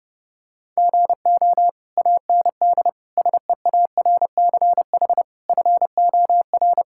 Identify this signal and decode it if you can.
Morse code, heard as a keyed tone, about 32 wpm, 700 Hz. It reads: GO AND SEARCH FOR
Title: Sentences from Top 1000 Words - Review Part 5 - 30wpm